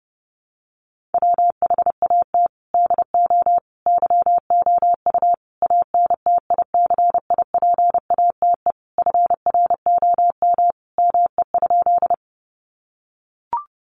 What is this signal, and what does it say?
Morse code, heard as a keyed tone, about 30 wpm, 700 Hz. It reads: WHAT DO YOU ANTICIPATE FROM ME?
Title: Sentences from Top 1500 Words 30wpm